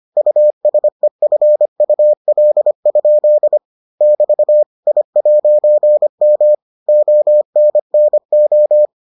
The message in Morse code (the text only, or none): USEFUL? <BT> I'M ONNO